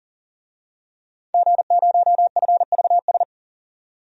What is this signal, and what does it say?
Morse code, heard as a keyed tone, about 39 wpm, 700 Hz. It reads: G0FVS
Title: DXCC-100 Call Signs - 5 Characters - 40wpm